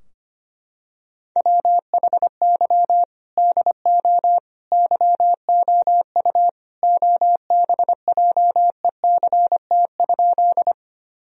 Morse code, heard as a keyed tone, about 25 wpm, 700 Hz. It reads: WHY DO YOU OBJECT?
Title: Sentences from Top 1000 Words - Review Part 3 - 25wpm